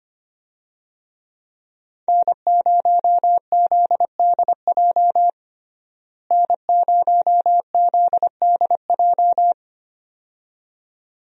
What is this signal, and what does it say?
Morse code, heard as a keyed tone, about 25 wpm, 700 Hz. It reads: N0ZDJ N0ZDJ
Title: USA Call Signs - 5 Characters - Repeated 2x - 25wpm